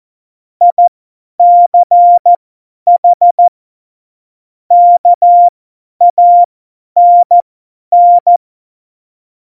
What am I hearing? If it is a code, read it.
Morse code, heard as a keyed tone, about 14 wpm, spaced as for 10 wpm, 700 Hz. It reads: ICH KANN
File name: Morsekurs / Witze / Witz 20 March 2018, 50 F70